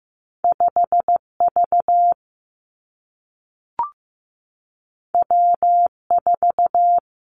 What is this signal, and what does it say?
Morse code, heard as a keyed tone, about 15 wpm, 700 Hz. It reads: 5 V W 4
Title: Two Challenging Letter-Number Combinations 15wpm